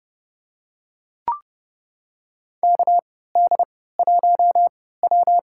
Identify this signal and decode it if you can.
Morse code, heard as a keyed tone, about 30 wpm, spaced as for 17 wpm, 700 Hz. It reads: KD1W
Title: USA Call Signs - 3x Character Spacing - 30wpm